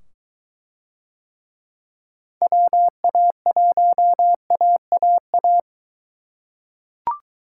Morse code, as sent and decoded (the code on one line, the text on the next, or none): .-- .- .---- .- .- .-
WA1AAA